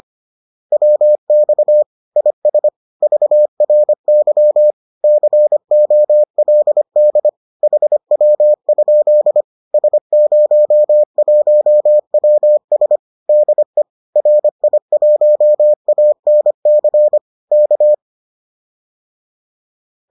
Morse code, heard as a keyed tone, about 25 wpm, 600 Hz. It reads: WX IS VRY COLD HW? S01WS DE RI1ANC K
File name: 25wpm-6